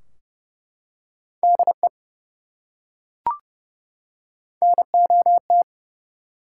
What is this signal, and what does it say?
Morse code, heard as a keyed tone, about 30 wpm, 700 Hz. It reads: DE NOT
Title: Top 180 QSO Elements 30wpm